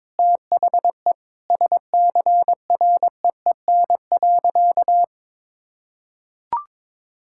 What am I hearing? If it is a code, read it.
Morse code, heard as a keyed tone, about 21 wpm, 700 Hz. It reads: THE SCREEN.
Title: Sentences from Top 1000 Words - Review Part 6 - 22wpm